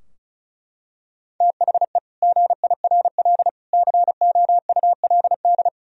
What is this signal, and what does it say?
Morse code, heard as a keyed tone, about 36 wpm, 700 Hz. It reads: THE GIRL COULD
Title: Sentences from Top 1000 Words - Review Part 4 - 35wpm